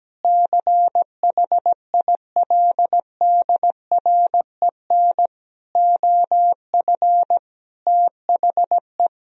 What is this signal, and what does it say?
Morse code, heard as a keyed tone, about 17 wpm, 700 Hz. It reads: CHILDREN OF THE
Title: Sentences from Top 1000 Words - Review Part 4 - 17wpm